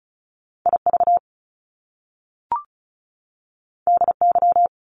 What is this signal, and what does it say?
Morse code, heard as a keyed tone, about 36 wpm, 700 Hz. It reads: IV DY
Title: Top Two Letter Combinations in Words 35wpm